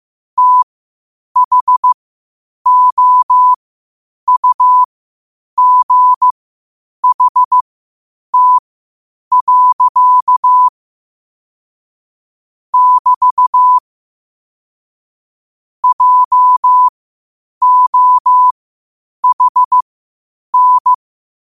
Morse code, heard as a keyed tone, about 15 wpm, spaced as for 8 wpm, 1000 Hz. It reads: THOUGHT. <BT> JOHN
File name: 8wpm